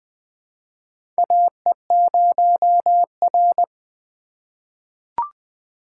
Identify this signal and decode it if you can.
Morse code, heard as a keyed tone, about 20 wpm, 700 Hz. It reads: AE0R
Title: USA Call Signs - 4 Characters - 3x Character Spacing - 20wpm